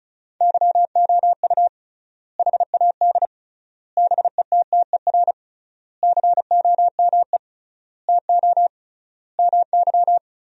Morse code, heard as a keyed tone, about 36 wpm, 700 Hz. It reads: YOU HAD BETTER COME TO MY
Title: 3x Word-Spacing - Sentences from Top 400 Words - 35wpm